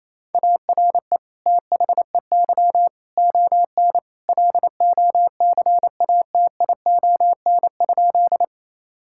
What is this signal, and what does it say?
Morse code, heard as a keyed tone, about 28 wpm, 700 Hz. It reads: ARE THEY ON LOCATION?
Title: Sentences from Top 1000 Words - Review Part 6 - 28wpm